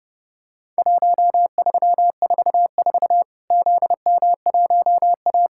Morse code, heard as a keyed tone, about 32 wpm, 700 Hz. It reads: 1344 ZM1A
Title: Worked All Europe DX Contest - QTC Exchange - 30wpm